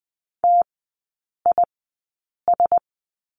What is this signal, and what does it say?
Morse code, heard as a keyed tone, about 20 wpm, spaced as for 8 wpm, 700 Hz. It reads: TIS
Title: 035 Lesson 7e - Letter s - Three Characters 20wpm